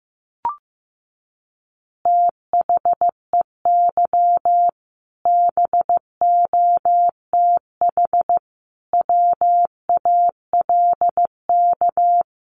Morse code, heard as a keyed tone, about 15 wpm, 700 Hz. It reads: THEY BOTH WALK
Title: Additional Easy Phrases 15wpm